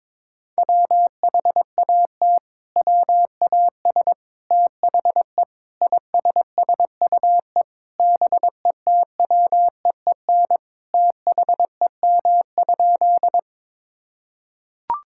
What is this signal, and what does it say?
Morse code, heard as a keyed tone, about 22 wpm, 700 Hz. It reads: WHAT WAS THE ISSUE BETWEEN THEM?
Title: Sentences from Top 1000 Words - Review Part 6 - 22wpm